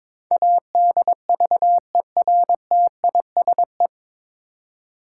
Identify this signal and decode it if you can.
Morse code, heard as a keyed tone, about 22 wpm, 700 Hz. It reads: ADVERTISE
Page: Morse Code Ninja practice 22wpm